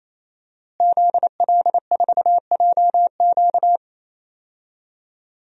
Morse code, heard as a keyed tone, about 28 wpm, 690 Hz. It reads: ZL4JQ
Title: DXCC-100 Call Signs - 5 Characters - Repeated 3x - 28wpm